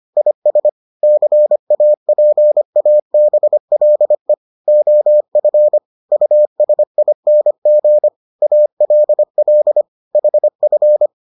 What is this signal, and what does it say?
Morse code, heard as a keyed tone, about 25 wpm, 600 Hz. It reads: IS CAPABLE OF USING ALL HF